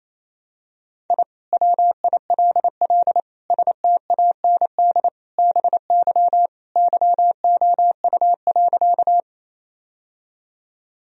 Morse code, heard as a keyed tone, about 28 wpm, 700 Hz. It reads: I WILL STAND BY YOU.
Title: Sentences from Top 1000 Words - Review Part 1 - 28wpm